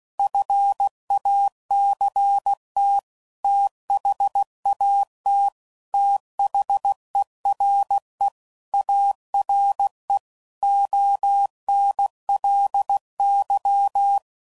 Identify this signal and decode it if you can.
Morse code, heard as a keyed tone, about 16 wpm, 800 Hz. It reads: FACT THAT THERE ARE ONLY
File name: CW Code Groups @15WPM / Technical Subjects / AND Gate